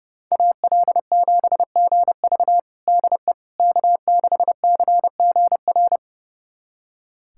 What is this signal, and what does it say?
Morse code, heard as a keyed tone, about 30 wpm, 700 Hz. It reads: AL7GV DE K6CGR
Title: USA Callsign DE Callsign - 5 Characters - 30wpm